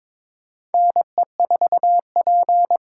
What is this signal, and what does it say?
Morse code, heard as a keyed tone, about 22 wpm, 700 Hz. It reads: NE4P
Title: USA Call Signs - 4 Characters - 4x Character Spacing - 22wpm